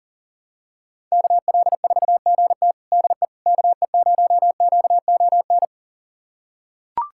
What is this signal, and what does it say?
Morse code, heard as a keyed tone, about 40 wpm, 700 Hz. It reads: KL4GT DE KE0QON
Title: USA Callsign DE Callsign 40wpm